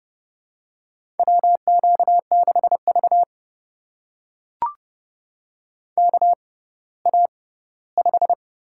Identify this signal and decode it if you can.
Morse code, heard as a keyed tone, about 31 wpm, spaced as for 17 wpm, 700 Hz. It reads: WQ6V KA5
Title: USA Call Signs - 4 Characters - 6x Character Spacing - 30wpm